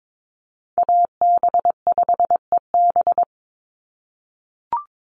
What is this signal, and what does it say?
Morse code, heard as a keyed tone, about 22 wpm, 700 Hz. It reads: AB5EB
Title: Call Signs - 22wpm